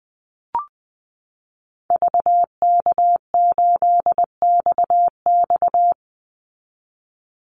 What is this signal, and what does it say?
Morse code, heard as a keyed tone, about 20 wpm, 700 Hz. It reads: VK8XX
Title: Call Signs - 20wpm